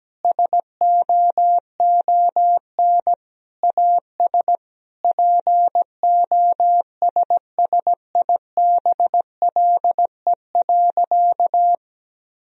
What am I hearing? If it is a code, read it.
Morse code, heard as a keyed tone, about 17 wpm, 700 Hz. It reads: SOON AS POSSIBLE.
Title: Sentences from Top 1000 Words - Review Part 4 - 17wpm